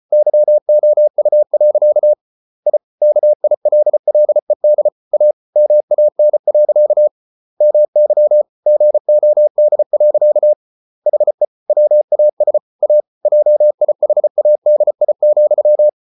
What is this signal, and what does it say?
Morse code, heard as a keyed tone, about 34 wpm, 600 Hz. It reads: YOU. I KILLED A MAN. MY GOD. HE WAS A JIHADI,